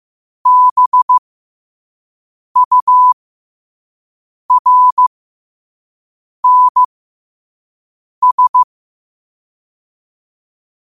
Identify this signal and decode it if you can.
Morse code, heard as a keyed tone, about 15 wpm, spaced as for 5 wpm, 1000 Hz. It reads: BURNS
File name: qotdfiles / 5wpm